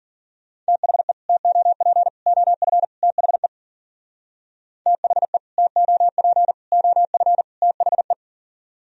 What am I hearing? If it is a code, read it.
Morse code, heard as a keyed tone, about 52 wpm, 700 Hz. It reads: THE TOP OF THE THE TOP OF THE